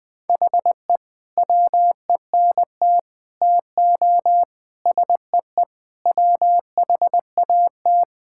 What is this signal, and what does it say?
Morse code, heard as a keyed tone, about 20 wpm, 700 Hz. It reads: HE WENT TO SEE WHAT
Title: Sentences with 8-Word Phrases - 20wpm